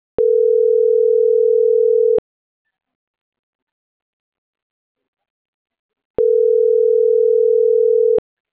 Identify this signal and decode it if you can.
Ringback tone (North America), 440 + 480 Hz.